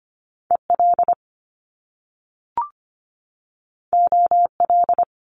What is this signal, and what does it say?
Morse code, heard as a keyed tone, about 25 wpm, 700 Hz. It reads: EL OL
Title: Top Two Letter Combinations in Words 25wpm